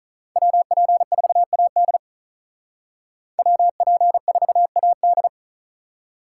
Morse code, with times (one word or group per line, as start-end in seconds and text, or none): 0.4-2.0 WP4AD
3.4-5.3 WP4AD